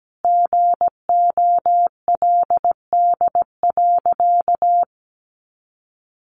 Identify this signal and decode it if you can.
Morse code, heard as a keyed tone, about 17 wpm, 700 Hz. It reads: GOLD.
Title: Sentences from Top 1200 Words 17wpm